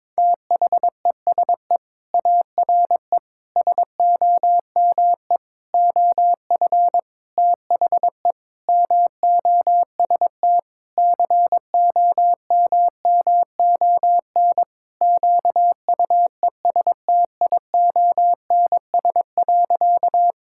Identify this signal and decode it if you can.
Morse code, heard as a keyed tone, about 22 wpm, 700 Hz. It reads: THESE ARE SOME OF THE MOST COMMON QUESTIONS.
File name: practice / 22wpm / Sentences with 4-Word Phrases - 22wpm